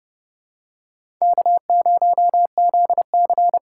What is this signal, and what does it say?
Morse code, heard as a keyed tone, about 31 wpm, 700 Hz. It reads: K0ZC
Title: USA Call Signs - 4 Characters - 2x Character Spacing - 30wpm